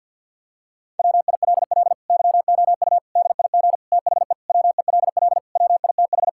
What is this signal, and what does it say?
Morse code, heard as a keyed tone, about 50 wpm, 700 Hz. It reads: WILL YOU DIG THE WELL WITH
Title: Sentences from Top 1600 Words 50wpm